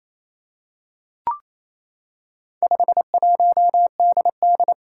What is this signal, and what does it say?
Morse code, heard as a keyed tone, about 28 wpm, 700 Hz. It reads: 51DD
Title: Four Challenging Letter-Number Combinations 28wpm